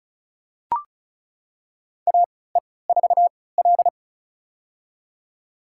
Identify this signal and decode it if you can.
Morse code, heard as a keyed tone, about 36 wpm, spaced as for 20 wpm, 700 Hz. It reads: AE4L
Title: USA Call Signs - 3x Character Spacing - 35wpm